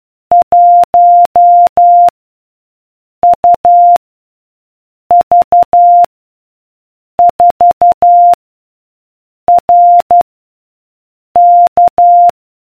Morse code, heard as a keyed tone, about 12 wpm, 700 Hz. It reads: 1 U V 4 R K